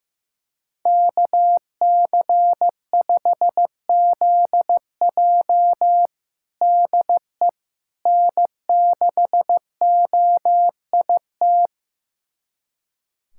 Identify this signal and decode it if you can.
Morse code, heard as a keyed tone, about 15 wpm, 700 Hz. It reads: KC5ZJ DE N6OIT
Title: USA Callsign DE Callsign - 5 Characters - 15wpm